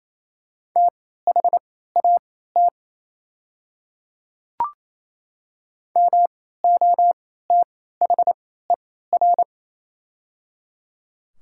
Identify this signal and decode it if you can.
Morse code, heard as a keyed tone, about 28 wpm, spaced as for 16 wpm, 700 Hz. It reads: THAT MOTHER
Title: 3x Character Spacing - Top 300 Words - 28wpm